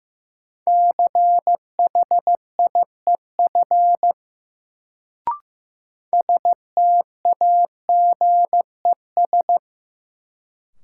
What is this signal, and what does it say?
Morse code, heard as a keyed tone, about 15 wpm, 700 Hz. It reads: CHIEF STAGES
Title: Sets of 5 Words Encore - New Word Drill - 15wpm